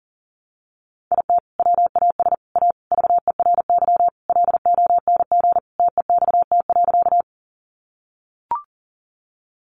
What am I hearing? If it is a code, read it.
Morse code, heard as a keyed tone, about 40 wpm, 700 Hz. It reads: IT WAS A VERY LONG TEXT.
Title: Sentences from Top 1200 Words 40wpm